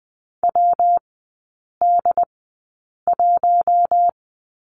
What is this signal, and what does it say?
Morse code, heard as a keyed tone, about 20 wpm, spaced as for 8 wpm, 700 Hz. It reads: WD1
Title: 127 Lesson 21e - Number 6 - Three Characters 20wpm